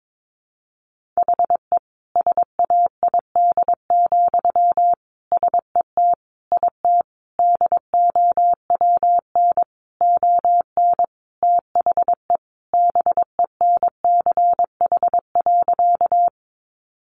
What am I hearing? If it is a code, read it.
Morse code, heard as a keyed tone, about 22 wpm, 700 Hz. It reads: HE SAID, SET IT DOWN ON THE BENCH.
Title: Sentences from Top 1400 Words 22wpm